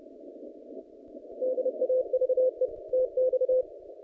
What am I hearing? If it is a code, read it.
Morse code, heard as a keyed tone, about 30 wpm, 510 Hz. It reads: DAVE TX
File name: contests / naqp-cw-k5tr-op-k5pi- -0300